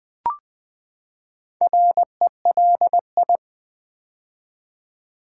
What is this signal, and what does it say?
Morse code, heard as a keyed tone, about 20 wpm, 700 Hz. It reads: RELI
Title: Build A Word - 4 Letters At A Time - 8-Letter Words - 20wpm